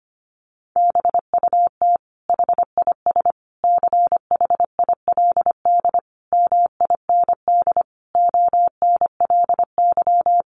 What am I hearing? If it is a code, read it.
Morse code, heard as a keyed tone, about 25 wpm, 700 Hz. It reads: BUT HIS CHILD MIND ONLY
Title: Sentences from Top 800 Words 25wpm